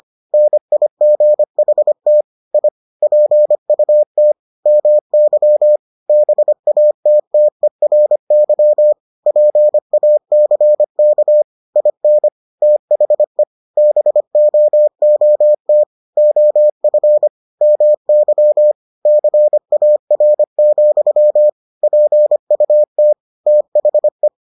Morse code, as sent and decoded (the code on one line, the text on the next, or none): -. .. --. .... - / .. / .--. ..- - / -- -.-- / -... .- - - . .-. -.-- / .--. .- -.-. -.- / .. -. / - .... / -... --- --- - / --- ..-. / -- -.-- / -.-. .- .-. --..-- / .--. ..- - / - .... .
NIGHT I PUT MY BATTERY PACK IN THE BOOT OF MY CAR, PUT THE